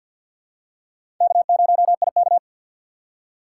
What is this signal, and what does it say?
Morse code, heard as a keyed tone, about 52 wpm, 700 Hz. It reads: K0IK
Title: USA Call Signs - 4x Character Spacing - 50wpm